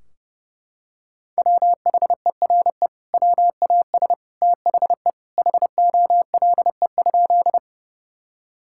Morse code, heard as a keyed tone, about 30 wpm, 700 Hz. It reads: WHERE WAS THE HOLE?
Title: Sentences from Top 1600 Words 30wpm